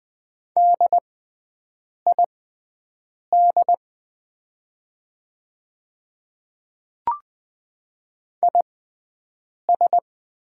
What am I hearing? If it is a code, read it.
Morse code, heard as a keyed tone, about 20 wpm, spaced as for 7 wpm, 700 Hz. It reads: DID IS
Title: 6x Character Spacing - Top 100 Words - 20wpm